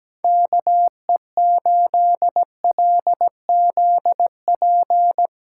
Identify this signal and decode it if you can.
Morse code, heard as a keyed tone, about 17 wpm, 700 Hz. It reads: KE8LZP
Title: USA Call Signs - 6 Characters - 8x Character Spacing - 17wpm